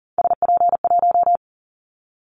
Morse code, heard as a keyed tone, about 40 wpm, 700 Hz. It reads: SP1
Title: Callsign Prefixes for Top 25 DXCC - 40wpm